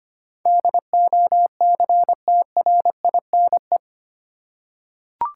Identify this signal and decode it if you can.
Morse code, heard as a keyed tone, about 25 wpm, 700 Hz. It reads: DOCTRINE
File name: Eight Letter Words - 25wpm